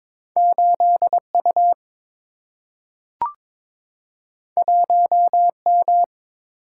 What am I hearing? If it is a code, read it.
Morse code, heard as a keyed tone, about 22 wpm, 700 Hz. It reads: 8U 1M
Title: Single-Letter Single-Number 22wpm